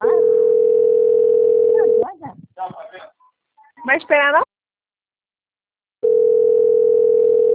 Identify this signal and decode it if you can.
Ringback tone (North America), 440 + 480 Hz, with speech.
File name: out- - - - - .35045